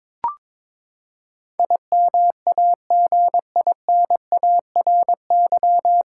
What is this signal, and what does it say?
Morse code, heard as a keyed tone, about 22 wpm, 700 Hz. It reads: IMAGINARY
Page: Morse Code Ninja practice 22wpm